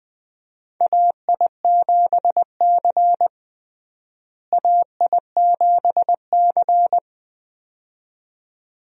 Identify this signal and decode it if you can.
Morse code, heard as a keyed tone, about 20 wpm, 700 Hz. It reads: AI7C AI7C
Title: USA Call Signs - 4 Characters - Repeated 2x - 20wpm